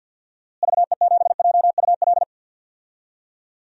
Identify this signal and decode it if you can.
Morse code, heard as a keyed tone, about 51 wpm, 700 Hz. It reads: VE7JUL
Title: DXCC-100 Call Signs - 6 Characters - Repeated 3x - 50wpm